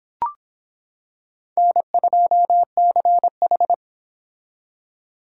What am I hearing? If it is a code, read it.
Morse code, heard as a keyed tone, about 24 wpm, 700 Hz. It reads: N2CH